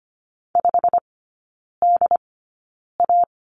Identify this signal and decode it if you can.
Morse code, heard as a keyed tone, about 25 wpm, spaced as for 9 wpm, 690 Hz. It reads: 5DA
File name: Lesson 15e - Number 5 - Three Characters 25wpm